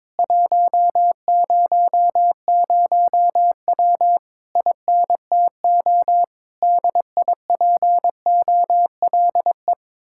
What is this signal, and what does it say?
Morse code, heard as a keyed tone, about 22 wpm, 700 Hz. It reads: 100W INTO DIPOLE
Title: Speed-Racing Common QSO Phrases - 22wpm